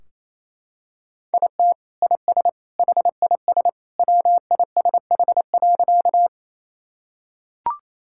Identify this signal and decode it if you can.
Morse code, heard as a keyed tone, about 28 wpm, 700 Hz. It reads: IT IS HIS WISH.
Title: Sentences from Top 1600 Words 28wpm